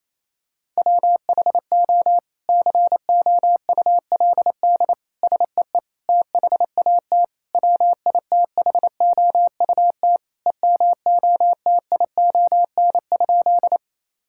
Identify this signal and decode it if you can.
Morse code, heard as a keyed tone, about 28 wpm, 700 Hz. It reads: WHO COULD SEE THAT WITHOUT EMOTION?
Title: Sentences from Top 1600 Words 28wpm